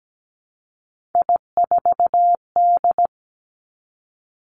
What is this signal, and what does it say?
Morse code, heard as a keyed tone, about 17 wpm, 700 Hz. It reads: I4D
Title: Three Challenging Letter-Number Combinations 17wpm